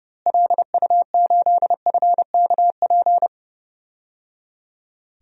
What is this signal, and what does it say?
Morse code, heard as a keyed tone, about 30 wpm, 700 Hz. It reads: LU8FKP
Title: DXCC-100 Call Signs - 6 Characters - Repeated 3x - 30wpm